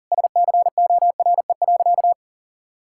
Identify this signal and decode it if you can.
Morse code, heard as a keyed tone, about 40 wpm, 700 Hz. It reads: SCORE.
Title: Sentences from Top 1600 Words 40wpm